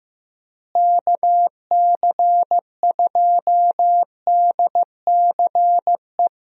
Morse code, heard as a keyed tone, about 15 wpm, 700 Hz. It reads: KC2DCE
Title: USA Call Signs - 6 Characters - Repeated 3x - 15wpm